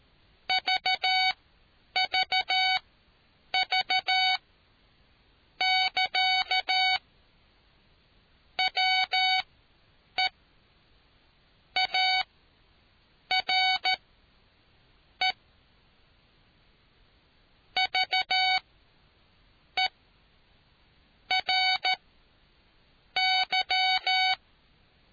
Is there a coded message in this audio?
Yes: Morse code from 0.5 to 24.3 s, heard as a keyed tone, about 14 wpm, spaced as for 6 wpm, 750 Hz. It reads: VVV<KA> WE ARE VERY